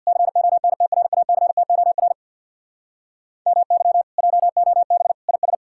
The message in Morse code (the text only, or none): CONTRACTOR MY JOB IS